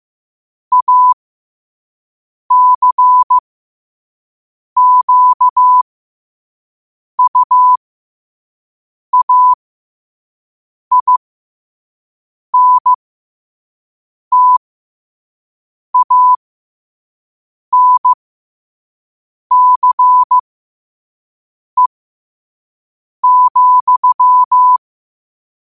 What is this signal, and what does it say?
Morse code, heard as a keyed tone, about 15 wpm, spaced as for 5 wpm, 1000 Hz. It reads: ACQUAINTANCE,
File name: qotdfiles / 5wpm